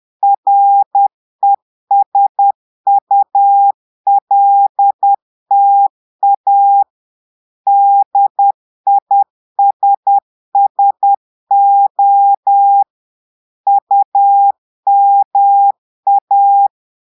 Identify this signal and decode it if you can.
Morse code, heard as a keyed tone, about 10 wpm, 800 Hz. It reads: RESULTA DISSO UMA
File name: Portugais - texte en clair 1 - vitesse - 10 mots minute